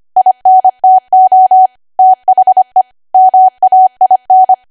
Morse code, heard as a keyed tone, about 25 wpm, 750 Hz. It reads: INTO THE MAIN
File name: Archive / 25 WPM / 25WPM